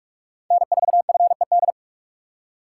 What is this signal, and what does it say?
Morse code, heard as a keyed tone, about 43 wpm, 700 Hz. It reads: N4FED